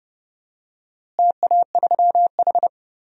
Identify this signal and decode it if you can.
Morse code, heard as a keyed tone, about 30 wpm, 700 Hz. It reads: TA3H
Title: DXCC-100 Call Signs - 4 Characters - Repeated 3x - 30wpm